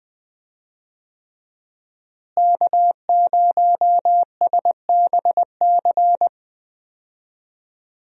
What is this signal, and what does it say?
Morse code, heard as a keyed tone, about 20 wpm, 700 Hz. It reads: K0SBC
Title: USA Call Signs - 5 Characters - Repeated 3x - 20wpm